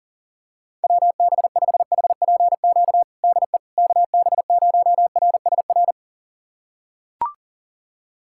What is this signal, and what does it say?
Morse code, heard as a keyed tone, about 40 wpm, 700 Hz. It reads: WB5HPQ DE KB0RSR
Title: USA Callsign DE Callsign - 6 Characters - 40wpm